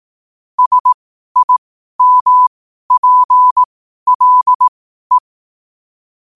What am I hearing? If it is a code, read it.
Morse code, heard as a keyed tone, about 18 wpm, spaced as for 12 wpm, 1000 Hz. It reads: SIMPLE
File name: qotdfiles / 12wpm